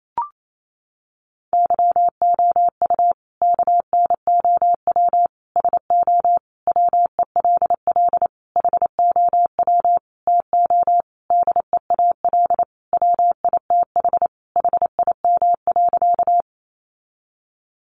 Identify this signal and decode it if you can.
Morse code, heard as a keyed tone, about 28 wpm, 700 Hz. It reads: YOU KNOW SO WELL HOW TO DEAL WITH HIM.
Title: Sentences from Top 800 Words 28wpm